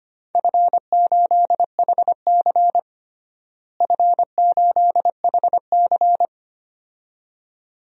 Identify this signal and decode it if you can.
Morse code, heard as a keyed tone, about 25 wpm, 700 Hz. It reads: F8HC F8HC
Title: DXCC-100 Call Signs - 4 Characters - Repeated 2x - 25wpm